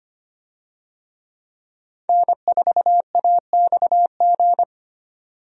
Morse code, heard as a keyed tone, about 25 wpm, 700 Hz. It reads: N4AXG
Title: USA Call Signs - 5 Characters - 25wpm